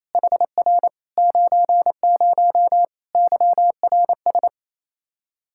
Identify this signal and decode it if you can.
Morse code, heard as a keyed tone, about 27 wpm, 700 Hz. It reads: HR 90 YRS
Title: Practice QSO - Exchange 3 - More Info - 28wpm